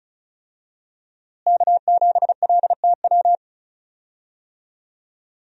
Morse code, heard as a keyed tone, about 34 wpm, 700 Hz. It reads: K7LTW
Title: USA Call Signs - 5 Characters - Repeated 2x - 35wpm